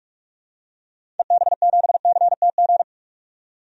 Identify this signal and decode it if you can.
Morse code, heard as a keyed tone, about 46 wpm, 700 Hz. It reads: EB7CTG
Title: DXCC-100 Call Signs - 6 Characters - 45wpm